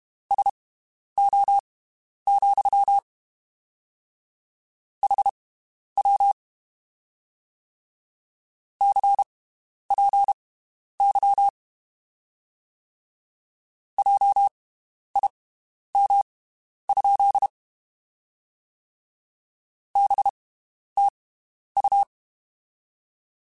Morse code, heard as a keyed tone, about 32 wpm, spaced as for 11 wpm, 800 Hz. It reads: SO, HW CPY JIM? BTU
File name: (spaced) / QSOs Spaced-mp3s / WD0BGP 1 @30 sp